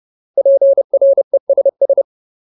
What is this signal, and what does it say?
Morse code, heard as a keyed tone, about 30 wpm, 550 Hz. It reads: PRESS
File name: Ultimate Word List / 30WPM Ultimate